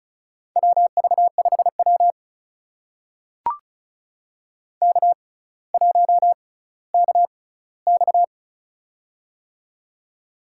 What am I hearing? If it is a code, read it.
Morse code, heard as a keyed tone, about 34 wpm, spaced as for 18 wpm, 700 Hz. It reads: WV5W K1KX
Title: USA Call Signs - 4 Characters - 6x Character Spacing - 35wpm